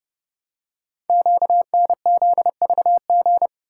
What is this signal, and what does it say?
Morse code, heard as a keyed tone, about 30 wpm, 700 Hz. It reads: QNZVG
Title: Code Groups - 5 Letters - 30wpm